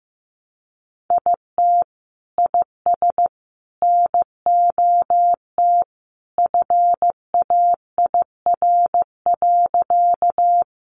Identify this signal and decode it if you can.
Morse code, heard as a keyed tone, about 15 wpm, 700 Hz. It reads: IT IS NOT FAIR.
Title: Sentences from Top 1300 Words 15wpm